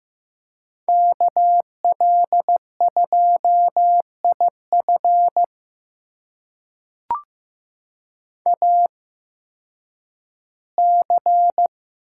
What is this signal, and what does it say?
Morse code, heard as a keyed tone, about 15 wpm, spaced as for 10 wpm, 700 Hz. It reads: KL2IF AC
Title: USA Call Signs - 5 Characters - 8x Character Spacing - 15wpm